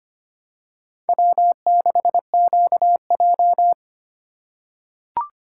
Morse code, heard as a keyed tone, about 25 wpm, 700 Hz. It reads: W6QJ
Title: USA Call Signs - 4 Characters - 3x Character Spacing - 25wpm